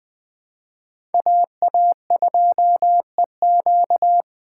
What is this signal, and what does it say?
Morse code, heard as a keyed tone, about 21 wpm, 700 Hz. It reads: AA2EQ
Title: Speed Racing - USA 2x2 Call Signs 17wpm